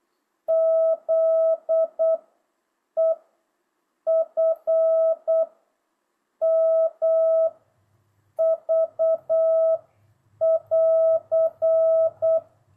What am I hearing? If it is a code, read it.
Morse code, heard as a keyed tone, about 8 wpm, 660 Hz. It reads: Z E F M V <AR>